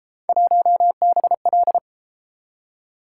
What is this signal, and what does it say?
Morse code, heard as a keyed tone, about 31 wpm, 700 Hz. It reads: 1BL